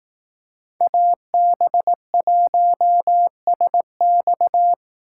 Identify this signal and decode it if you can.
Morse code, heard as a keyed tone, about 18 wpm, 700 Hz. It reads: AB1SX
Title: Speed Racing - USA Call Signs 15wpm